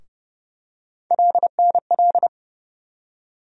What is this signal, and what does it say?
Morse code, heard as a keyed tone, about 30 wpm, 700 Hz. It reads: LNL
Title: USA Call Sign Postfixes - 3 Letters 30wpm